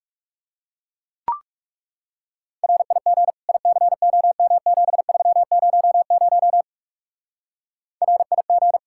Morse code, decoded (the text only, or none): RIG ICOM7300 RIG